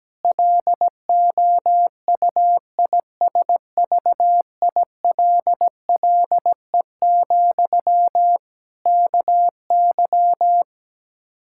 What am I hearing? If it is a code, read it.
Morse code, heard as a keyed tone, about 17 wpm, 700 Hz. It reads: LOUISVILLE, KY
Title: K1USN SST QSO Practice 17wpm